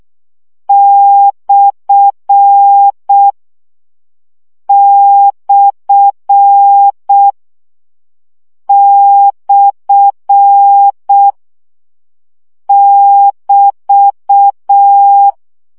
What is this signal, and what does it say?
Morse code, heard as a keyed tone, about 6 wpm, 800 Hz. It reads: / / / <BT>
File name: CW gyakorlas per 5 baud